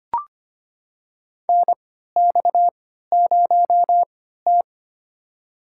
Morse code, heard as a keyed tone, about 25 wpm, spaced as for 14 wpm, 700 Hz. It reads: NX0T
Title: USA Call Signs - 4 Characters - 3x Character Spacing - 25wpm